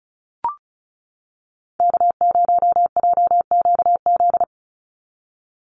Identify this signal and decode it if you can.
Morse code, heard as a keyed tone, about 34 wpm, 700 Hz. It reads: K0JQZ
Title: NA SOTA Chaser Callsigns 35wpm